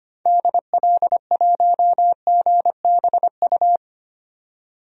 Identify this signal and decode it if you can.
Morse code, heard as a keyed tone, about 25 wpm, 700 Hz. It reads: DL1GBU
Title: DXCC-100 Call Signs - 6 Characters - 25wpm